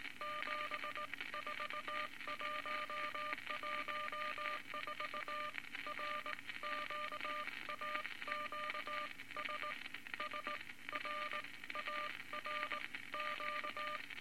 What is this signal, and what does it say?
Morse code, heard as a keyed tone, about 19 wpm, spaced as for 16 wpm, 1200 Hz, with noise. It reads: 74114 RQAQS SRARQ
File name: Morse Code Records / 19 B LCM 167 Twelve Words Per Minute